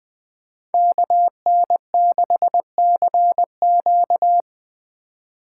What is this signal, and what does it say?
Morse code, heard as a keyed tone, about 21 wpm, 700 Hz. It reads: KN6CQ